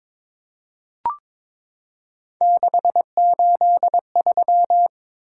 Morse code, heard as a keyed tone, about 22 wpm, 700 Hz. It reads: 683
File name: practice / 22wpm / Three Letter-Number Combinations - 22wpm